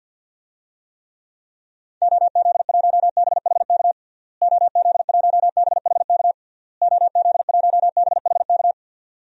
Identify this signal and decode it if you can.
Morse code, heard as a keyed tone, about 52 wpm, 700 Hz. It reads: OZ1BHK OZ1BHK OZ1BHK
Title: DXCC-100 Call Signs - 6 Characters - Repeated 3x - 50wpm